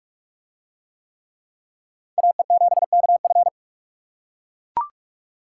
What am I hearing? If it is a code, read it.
Morse code, heard as a keyed tone, about 43 wpm, 700 Hz. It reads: AE7KF